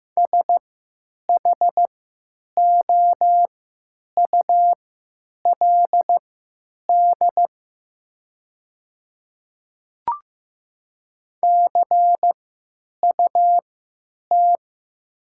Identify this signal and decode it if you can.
Morse code, heard as a keyed tone, about 15 wpm, spaced as for 9 wpm, 700 Hz. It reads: SHOULD CUT
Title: 3x Character Spacing - Top 500 Words - 15wpm